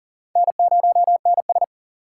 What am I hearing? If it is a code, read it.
Morse code, heard as a keyed tone, about 53 wpm, spaced as for 47 wpm, 700 Hz. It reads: N0NS
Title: Speed Racing - USA 1x2 Call Signs 40wpm